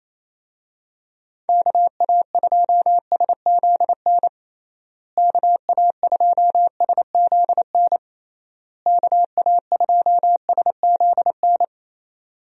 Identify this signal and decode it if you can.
Morse code, heard as a keyed tone, about 28 wpm, 700 Hz. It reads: KA2SZN KA2SZN KA2SZN
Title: USA Call Signs - 6 Characters - Repeated 3x - 28wpm